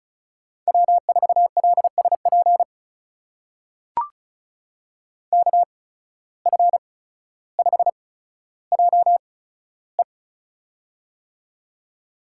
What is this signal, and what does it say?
Morse code, heard as a keyed tone, about 36 wpm, spaced as for 15 wpm, 700 Hz. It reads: W4LSP KF5JE
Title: USA Call Signs - 5 Characters - 8x Character Spacing - 35wpm